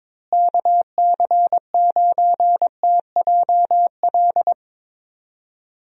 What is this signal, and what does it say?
Morse code, heard as a keyed tone, about 21 wpm, 700 Hz. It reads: KC9TJL
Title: USA Call Signs - 6 Characters - Repeated 3x - 22wpm